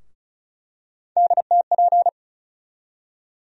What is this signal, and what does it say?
Morse code, heard as a keyed tone, about 34 wpm, 700 Hz. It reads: DTP
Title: USA Call Sign Postfixes - 3 Letters 35wpm